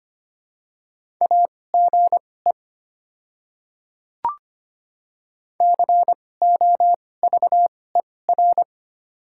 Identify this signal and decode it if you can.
Morse code, heard as a keyed tone, about 25 wpm, spaced as for 18 wpm, 700 Hz. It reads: AGE COVER
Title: 2x Character Spacing - Top 500 Words - 25wpm